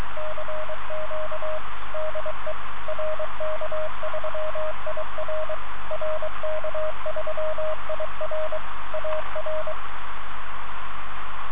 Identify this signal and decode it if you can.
Morse code, heard as a keyed tone, about 23 wpm, 620 Hz, with radio noise. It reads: CQ DE RK3IR RK3IR AR